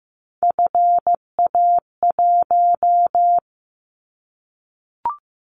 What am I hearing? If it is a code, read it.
Morse code, heard as a keyed tone, about 15 wpm, 700 Hz. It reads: FA1
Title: Callsign Prefixes for Top 50 DXCC - 15wpm